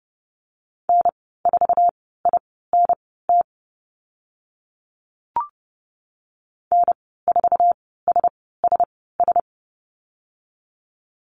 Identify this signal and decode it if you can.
Morse code, heard as a keyed tone, about 30 wpm, spaced as for 17 wpm, 700 Hz. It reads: N4INT N4SSS
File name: Lesson 15g - Number 5 - Call Signs 30wpm